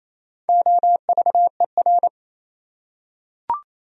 Morse code, heard as a keyed tone, about 28 wpm, 700 Hz. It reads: OVER
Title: Top 100 Words 28wpm